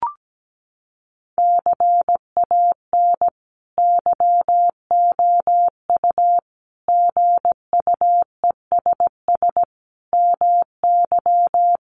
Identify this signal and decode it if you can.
Morse code, heard as a keyed tone, about 17 wpm, 700 Hz. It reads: CAN YOU GUESS MY
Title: Sentences from Top 1500 Words 17wpm